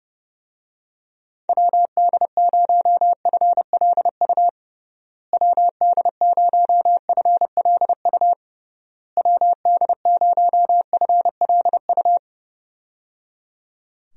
Morse code, heard as a keyed tone, about 29 wpm, 700 Hz. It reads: WD0FLU WD0FLU WD0FLU
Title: USA Call Signs - 6 Characters - Repeated 3x - 30wpm